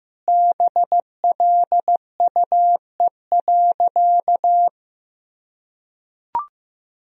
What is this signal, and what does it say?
Morse code, heard as a keyed tone, about 16 wpm, 700 Hz. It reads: BLUE.
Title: Sentences from Top 1000 Words - Review Part 3 - 15wpm